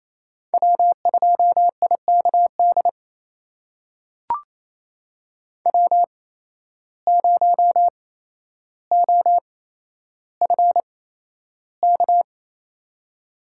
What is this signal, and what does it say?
Morse code, heard as a keyed tone, about 28 wpm, spaced as for 12 wpm, 700 Hz. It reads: W2IKD W0OFK
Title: USA Call Signs - 5 Characters - 8x Character Spacing - 28wpm